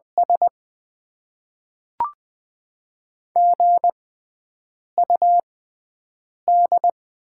Morse code, heard as a keyed tone, about 20 wpm, spaced as for 7 wpm, 690 Hz. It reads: S GUD
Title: Top 20 QSO Elements - 6x Character Spacing - 20wpm